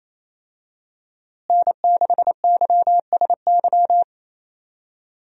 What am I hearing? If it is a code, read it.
Morse code, heard as a keyed tone, about 28 wpm, 700 Hz. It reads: N6YSY
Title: USA Call Signs - 5 Characters - Repeated 2x - 28wpm